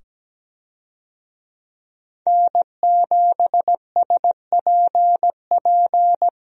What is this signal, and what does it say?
Morse code, heard as a keyed tone, about 17 wpm, 700 Hz. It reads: N7SPP
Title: USA Call Signs - 5 Characters - 17wpm